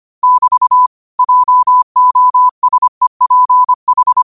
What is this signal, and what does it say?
Morse code, heard as a keyed tone, about 25 wpm, 1000 Hz. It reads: <BT> JOSEPH
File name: qotdfiles / 25wpm